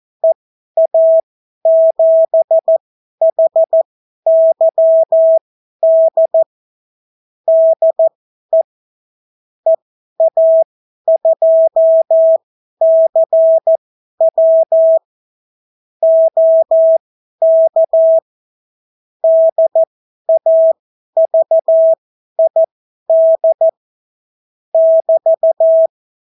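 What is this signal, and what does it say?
Morse code, heard as a keyed tone, about 14 wpm, spaced as for 11 wpm, 650 Hz. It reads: EA7HYD DE EA2CW OK DAVID <BT>
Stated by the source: Example Spanish CW QSO from a Morse learning book, 14 wpm characters with 8 wpm spacing.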